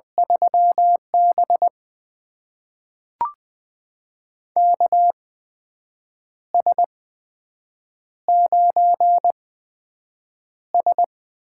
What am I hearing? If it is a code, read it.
Morse code, heard as a keyed tone, about 20 wpm, spaced as for 7 wpm, 700 Hz. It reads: 3B KS9S
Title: USA Call Signs - 4 Characters - 8x Character Spacing - 20wpm